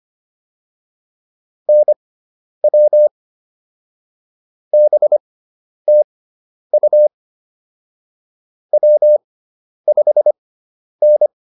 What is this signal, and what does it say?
Morse code, heard as a keyed tone, about 25 wpm, spaced as for 10 wpm, 600 Hz. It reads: NW BTU W5N